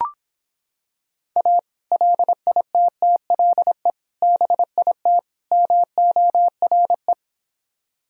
Morse code, heard as a keyed tone, about 26 wpm, 700 Hz. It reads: A LITTLE BIT MORE